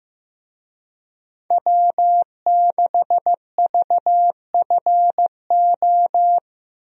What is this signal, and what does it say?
Morse code, heard as a keyed tone, about 16 wpm, 700 Hz. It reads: W6VFO
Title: USA Call Signs - 5 Characters - 15wpm